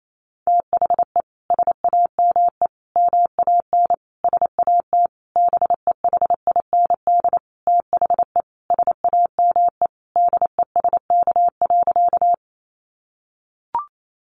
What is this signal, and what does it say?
Morse code, heard as a keyed tone, about 28 wpm, 700 Hz. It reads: THE SAME MAN SAT BEHIND THE SAME DESK.
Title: A Sound Of Thunder 28wpm